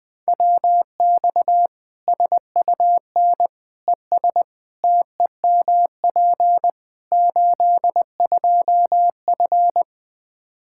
Morse code, heard as a keyed tone, about 21 wpm, 700 Hz. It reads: WX SUN ES TEMP 82F
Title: K1USN SST QSO Practice 20wpm